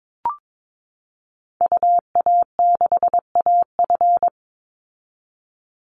Morse code, heard as a keyed tone, about 22 wpm, 700 Hz. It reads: UA6AF
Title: Call Signs Repeated 2x - 22wpm